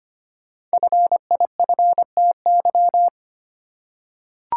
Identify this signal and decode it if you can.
Morse code, heard as a keyed tone, about 25 wpm, 700 Hz. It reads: FIFTY
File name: 5 Letter Words - First 3 Letters Given - 25wpm